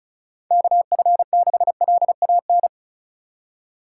Morse code, heard as a keyed tone, about 36 wpm, 700 Hz. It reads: KF6LAN
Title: USA Call Signs - 6 Characters - 2x Character Spacing - 35wpm